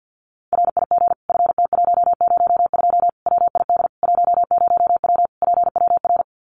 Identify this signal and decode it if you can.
Morse code, heard as a keyed tone, about 50 wpm, 700 Hz. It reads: RIG FT102 WID 10W PWR
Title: Common QSO Phrases 50wpm